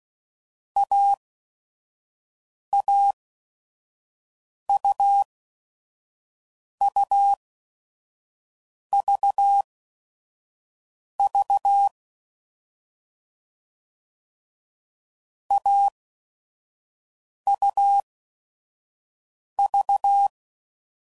Morse code, heard as a keyed tone, about 16 wpm, spaced as for 5 wpm, 800 Hz. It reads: AAUUVV AUV
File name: Mixed / Short Runs / AUV AUV4